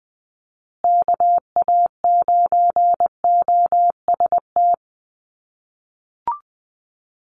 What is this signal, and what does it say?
Morse code, heard as a keyed tone, about 20 wpm, 700 Hz. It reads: KA9OST
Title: USA Call Signs Repeated 2x 20wpm